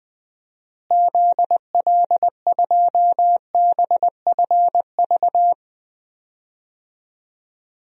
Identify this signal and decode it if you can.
Morse code, heard as a keyed tone, about 20 wpm, 700 Hz. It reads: ZL2BFV
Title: DXCC-100 Call Signs - 6 Characters - Repeated 3x - 20wpm